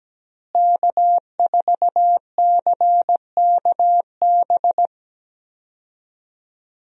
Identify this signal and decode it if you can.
Morse code, heard as a keyed tone, about 17 wpm, 700 Hz. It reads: K4CKB
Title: USA Call Signs - 5 Characters - Repeated 3x - 17wpm